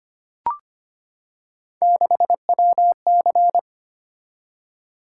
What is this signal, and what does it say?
Morse code, heard as a keyed tone, about 25 wpm, 700 Hz. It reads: 6WC
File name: Three Letter-Number Combinations - 25wpm